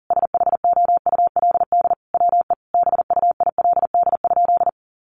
Morse code, heard as a keyed tone, about 40 wpm, 700 Hz. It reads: SHOULD WE BUILD?
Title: Sentences from Top 1300 Words 40wpm